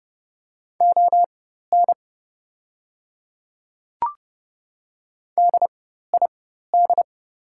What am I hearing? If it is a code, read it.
Morse code, heard as a keyed tone, about 30 wpm, spaced as for 14 wpm, 700 Hz. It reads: ON DID
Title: 4x Character Spacing - Top 100 Words - 30wpm